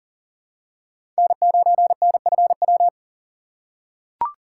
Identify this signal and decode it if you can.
Morse code, heard as a keyed tone, about 39 wpm, 700 Hz. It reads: N9NFW
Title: USA Call Signs - 5 Characters - 3x Character Spacing - 40wpm